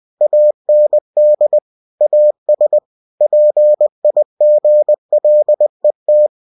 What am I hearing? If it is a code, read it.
Morse code, heard as a keyed tone, about 20 wpm, 600 Hz. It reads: AND AS PIGLET
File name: cw-books / winnie-the-pooh / winnie-20wpm-page-0058